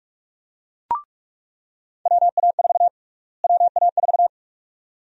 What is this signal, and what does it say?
Morse code, heard as a keyed tone, about 47 wpm, 700 Hz. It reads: WA4 WA4
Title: USA Call Sign Prefixes - Repeated 2x - 45wpm